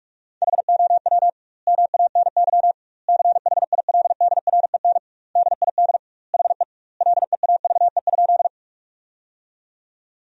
Morse code, heard as a keyed tone, about 42 wpm, 700 Hz. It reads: HOW MANY CHILDREN DID HE LEAVE?
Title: 2x Word-Spacing - Sentences from Top 400 Words - 45wpm